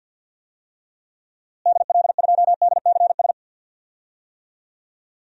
Morse code, heard as a keyed tone, about 53 wpm, 700 Hz. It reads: DL2DCS
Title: DXCC-100 Call Signs - 6 Characters - Repeated 3x - 50wpm